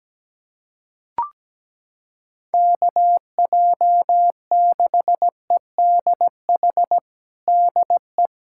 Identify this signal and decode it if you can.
Morse code, heard as a keyed tone, about 17 wpm, 700 Hz. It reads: KJ6EDH DE
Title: USA Callsign DE Callsign - 6 Characters - 17wpm